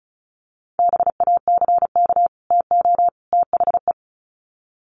Morse code, heard as a keyed tone, about 34 wpm, 700 Hz. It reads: BACK TO THE
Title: Sets of 3 Words 35wpm